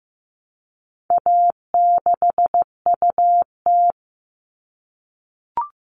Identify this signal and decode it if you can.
Morse code, heard as a keyed tone, about 15 wpm, 700 Hz. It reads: A6UT
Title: All Cut Numbers - 4-Digit 15wpm